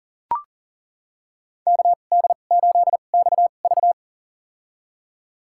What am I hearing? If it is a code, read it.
Morse code, heard as a keyed tone, about 40 wpm, spaced as for 29 wpm, 700 Hz. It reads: KD8XV
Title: USA Call Signs - 2x Character Spacing - 40wpm